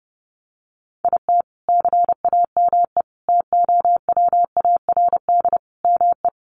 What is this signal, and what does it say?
Morse code, heard as a keyed tone, about 30 wpm, 700 Hz. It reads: IT CAME TOWARD ME
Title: Sentences from Top 1500 Words 30wpm